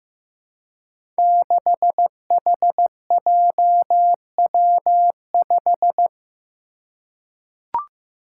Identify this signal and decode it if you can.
Morse code, heard as a keyed tone, about 15 wpm, 700 Hz. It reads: 6HJW5
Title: Five Challenging Letter-Number Combinations 15wpm